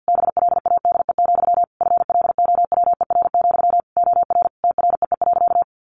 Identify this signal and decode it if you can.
Morse code, heard as a keyed tone, about 49 wpm, 700 Hz. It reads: BLADE, FLOWER, OR TREE.
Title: A Sound Of Thunder 50wpm